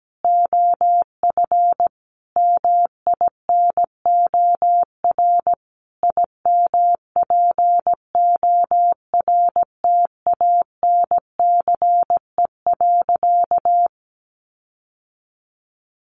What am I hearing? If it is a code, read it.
Morse code, heard as a keyed tone, about 17 wpm, 700 Hz. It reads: OF MINOR IMPORTANCE.
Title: Sentences from Top 1500 Words 17wpm